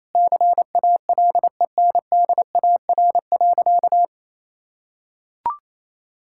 Morse code, heard as a keyed tone, about 28 wpm, 700 Hz. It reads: CALENDAR.
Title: Sentences from Top 1600 Words 28wpm